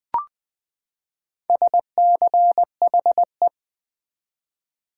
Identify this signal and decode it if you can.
Morse code, heard as a keyed tone, about 20 wpm, 700 Hz. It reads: SCHE
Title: Build A Word - 4 Letters At A Time - 8-Letter Words - 20wpm